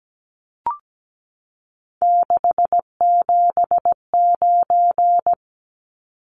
Morse code, heard as a keyed tone, about 17 wpm, 700 Hz. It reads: 679
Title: Numbers - 3-Digit 17wpm